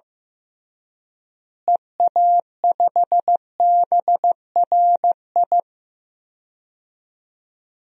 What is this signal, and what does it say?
Morse code, heard as a keyed tone, about 15 wpm, 700 Hz. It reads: EA5BRI
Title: DXCC-100 Call Signs - 6 Characters - Repeated 3x - 15wpm